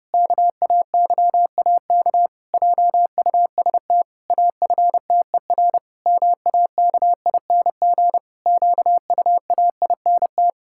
Morse code, heard as a keyed tone, about 30 wpm, 700 Hz. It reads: KAYAK JUST AFTER MAKING QUAINT
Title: Pangram Challenge v2 - 30wpm